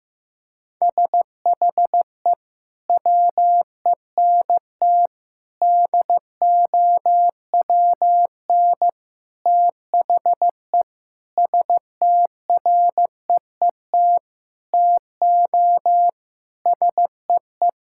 Morse code, heard as a keyed tone, about 15 wpm, 700 Hz. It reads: SHE WENT DOWN THE STREET TO SEE
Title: Sentences from Top 1000 Words - Review Part 5 - 15wpm